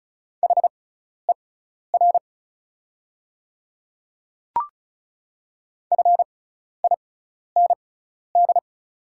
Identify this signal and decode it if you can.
Morse code, heard as a keyed tone, about 36 wpm, spaced as for 12 wpm, 700 Hz. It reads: HER FIND
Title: 6x Character Spacing - Top 100 Words - 35wpm